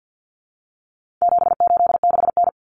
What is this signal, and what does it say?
Morse code, heard as a keyed tone, about 50 wpm, 700 Hz. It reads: 786N
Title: Common Cut Numbers - 4-Digit 50wpm